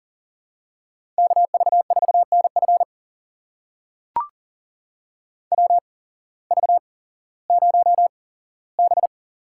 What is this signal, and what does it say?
Morse code, heard as a keyed tone, about 42 wpm, spaced as for 19 wpm, 700 Hz. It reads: KV4NF WV0B
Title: USA Call Signs - 8x Character Spacing - 40wpm